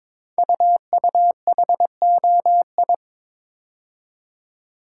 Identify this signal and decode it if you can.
Morse code, heard as a keyed tone, about 22 wpm, 700 Hz. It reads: UUHOI
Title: Code Groups - 5 Letters - 22wpm